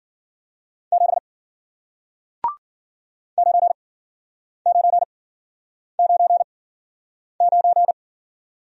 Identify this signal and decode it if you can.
Morse code, heard as a keyed tone, about 53 wpm, spaced as for 9 wpm, 700 Hz. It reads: 7 9999